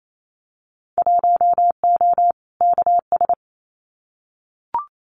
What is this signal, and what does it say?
Morse code, heard as a keyed tone, about 28 wpm, 700 Hz. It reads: 1O KS
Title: Winter Field Day Exchanges 28wpm